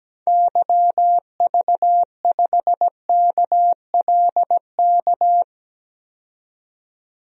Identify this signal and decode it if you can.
Morse code, heard as a keyed tone, about 17 wpm, 700 Hz. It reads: YV5KLK
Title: DXCC-100 Call Signs - 6 Characters - 17wpm